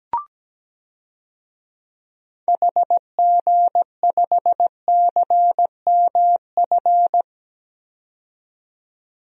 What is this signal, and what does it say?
Morse code, heard as a keyed tone, about 17 wpm, 700 Hz. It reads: HG5CMF
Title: DXCC-100 Call Signs - 6 Characters - 17wpm